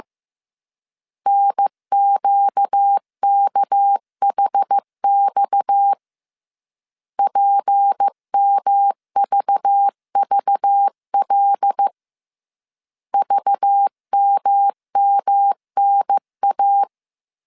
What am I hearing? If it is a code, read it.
Morse code, heard as a keyed tone, about 15 wpm, 800 Hz. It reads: NQKHX PMVVL VMMNA